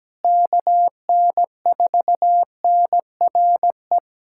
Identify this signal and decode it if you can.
Morse code, heard as a keyed tone, about 17 wpm, 700 Hz. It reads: KN4NRE
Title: USA Callsign DE Callsign - 6 Characters - 17wpm